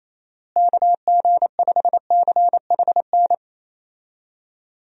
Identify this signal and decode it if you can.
Morse code, heard as a keyed tone, about 28 wpm, 700 Hz. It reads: KG5CHN
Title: USA Call Signs - 6 Characters - Repeated 3x - 28wpm